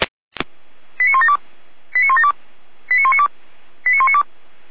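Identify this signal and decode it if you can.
Five-tone selective call (ZVEI), heard as quick runs of five tones, tones of about 80 ms. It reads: 89172 89172 89192 89192
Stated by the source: ZVEI five-tone selective call received by SDR.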